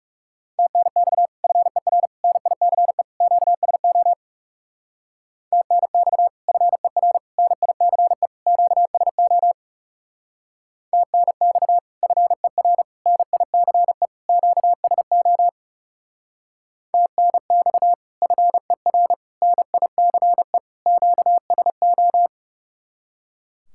Morse code, heard as a keyed tone, about 35 wpm, 700 Hz. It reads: TNX FER NICE QSO TNX FER NICE QSO TNX FER NICE QSO TNX FER NICE QSO
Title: Speed-Racing Common QSO Phrases Encore - 30wpm